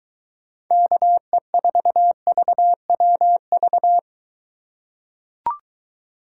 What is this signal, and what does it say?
Morse code, heard as a keyed tone, about 21 wpm, 700 Hz. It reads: KE4VWV